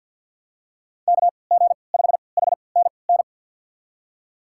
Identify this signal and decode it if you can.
Morse code, heard as a keyed tone, about 50 wpm, spaced as for 28 wpm, 690 Hz. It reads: KG5HNN
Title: USA Call Signs - 6 Characters - 3x Character Spacing - 50wpm